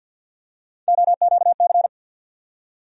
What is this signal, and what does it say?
Morse code, heard as a keyed tone, about 52 wpm, 700 Hz. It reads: OQC
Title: USA Call Sign Postfixes - 3 Letters 50wpm